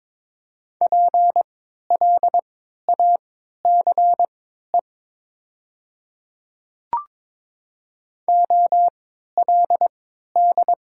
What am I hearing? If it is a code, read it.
Morse code, heard as a keyed tone, about 22 wpm, spaced as for 13 wpm, 700 Hz. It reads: PLACE OLD
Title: 3x Character Spacing - Top 200 Words - 22wpm